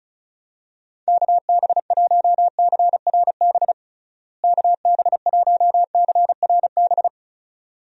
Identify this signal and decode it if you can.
Morse code, heard as a keyed tone, about 34 wpm, 700 Hz. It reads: KB1CRB KB1CRB
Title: USA Call Signs - 6 Characters - Repeated 2x - 35wpm